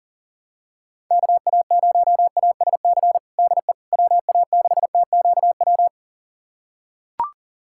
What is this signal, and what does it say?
Morse code, heard as a keyed tone, about 40 wpm, 700 Hz. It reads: KA0ASC DE WA6TQW
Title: USA Callsign DE Callsign - 6 Characters - 40wpm